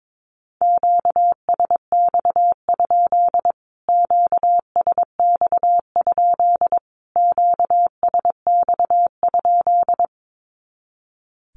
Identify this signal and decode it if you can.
Morse code, heard as a keyed tone, about 22 wpm, 700 Hz. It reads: QSX? QSX? QSX?
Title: Q-Codes Repeated 3x 22wpm